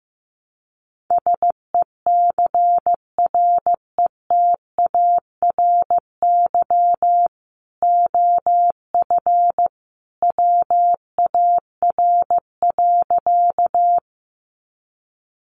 Morse code, heard as a keyed tone, about 15 wpm, 700 Hz. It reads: SECRETARY OF WAR.
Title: Sentences from Top 1300 Words 15wpm